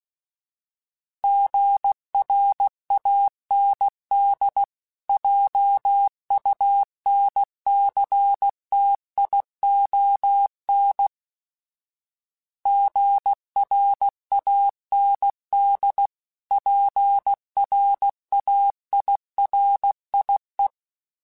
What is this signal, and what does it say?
Morse code, heard as a keyed tone, about 16 wpm, 800 Hz. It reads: GRAND JUNCTION GRAND PRAIRIE
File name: fromSMs / FromSMs mp3s / Cities@15